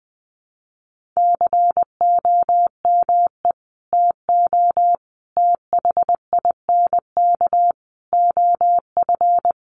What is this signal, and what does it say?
Morse code, heard as a keyed tone, about 20 wpm, 700 Hz. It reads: COME TO THINK OF
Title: Sets of 5 Words Encore 20wpm